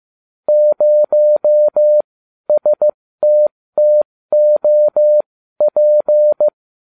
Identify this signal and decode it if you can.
Morse code, heard as a keyed tone, about 15 wpm, spaced as for 12 wpm, 600 Hz. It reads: 0STTOP